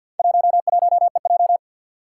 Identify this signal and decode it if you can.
Morse code, heard as a keyed tone, about 47 wpm, 700 Hz. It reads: 11EJ
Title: Four Challenging Letter-Number Combinations 50wpm